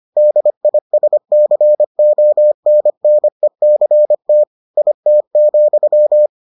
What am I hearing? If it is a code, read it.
Morse code, heard as a keyed tone, about 25 wpm, 600 Hz. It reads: DISCONNECT IT,